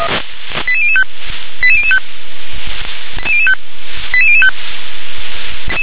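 Five-tone selective call (ZVEI), heard as quick runs of five tones, tones of about 70 ms. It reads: 88005 88005 88005
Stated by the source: ZVEI five-tone call 88058 received by SDR.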